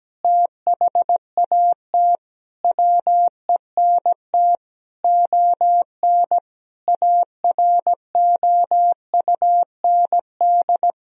Morse code, heard as a keyed tone, about 17 wpm, 700 Hz. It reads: THAT WENT ON AROUND
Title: Sentences from Top 1000 Words - Review Part 3 - 17wpm